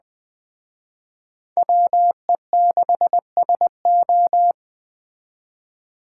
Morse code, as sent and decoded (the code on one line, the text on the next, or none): .-- . -.... ... ---
WE6SO